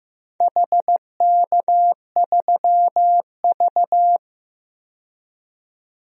Morse code, heard as a keyed tone, about 15 wpm, 700 Hz. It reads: HK3V
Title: DXCC-100 Call Signs - 4 Characters - 15wpm